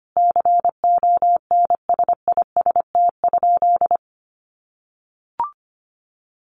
Morse code, heard as a keyed tone, about 25 wpm, 700 Hz. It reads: CONSIST?
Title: Sentences from Top 1400 Words 25wpm